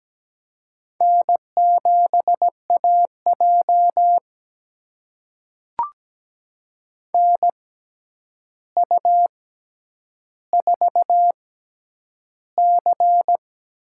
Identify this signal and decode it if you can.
Morse code, heard as a keyed tone, about 17 wpm, spaced as for 9 wpm, 700 Hz. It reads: N7AJ NU4C
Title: USA Call Signs - 4 Characters - 6x Character Spacing - 17wpm